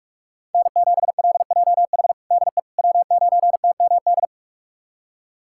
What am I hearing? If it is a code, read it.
Morse code, heard as a keyed tone, about 46 wpm, 700 Hz. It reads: N7LJH DE W9TMD
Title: USA Callsign DE Callsign - 5 Characters - 45wpm